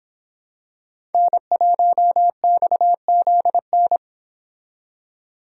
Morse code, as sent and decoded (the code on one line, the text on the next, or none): -. .---- -..- --.. -.
N1XZN